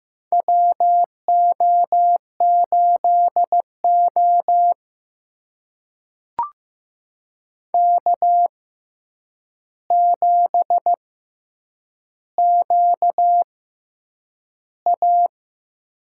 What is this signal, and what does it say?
Morse code, heard as a keyed tone, about 15 wpm, spaced as for 8 wpm, 700 Hz. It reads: WO8O K7QA
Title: USA Call Signs - 4 Characters - 6x Character Spacing - 15wpm